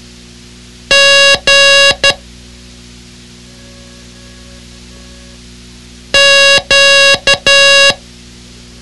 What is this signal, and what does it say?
Morse code, heard as a keyed tone, about 12 wpm, 2800 Hz. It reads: G Q
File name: ham radio / CodeQuick / Code Quick 06